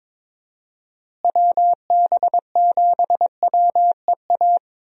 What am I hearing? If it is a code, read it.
Morse code, heard as a keyed tone, about 22 wpm, 700 Hz. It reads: WB7WEA
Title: USA Call Signs - 6 Characters - Repeated 2x - 22wpm